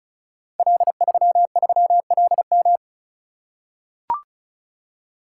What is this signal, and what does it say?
Morse code, heard as a keyed tone, about 36 wpm, 690 Hz. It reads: L33LM
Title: Code Groups - 5 Letter:Numbers - 35wpm